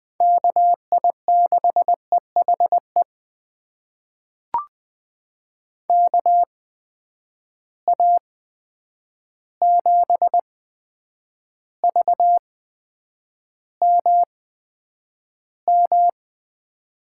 Morse code, heard as a keyed tone, about 20 wpm, spaced as for 9 wpm, 700 Hz. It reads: KI6EHE KA7VMM
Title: USA Call Signs - 6 Characters - 8x Character Spacing - 20wpm